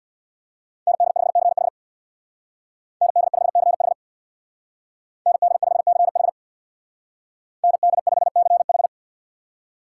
Morse code, heard as a keyed tone, about 60 wpm, 700 Hz. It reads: ND5CH ND5CH ND5CH ND5CH
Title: Speed Racing - USA 2x2 Call Signs 50wpm